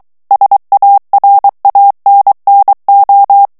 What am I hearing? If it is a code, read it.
Morse code, heard as a keyed tone, about 23 wpm, 800 Hz. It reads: SARANNO
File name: ITALIEN texte clair 2 vitesse 22 mots minute